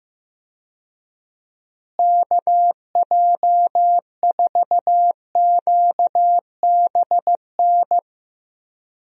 Morse code, heard as a keyed tone, about 15 wpm, 700 Hz. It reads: KJ4QBN
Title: USA Call Signs - 6 Characters - Repeated 2x - 15wpm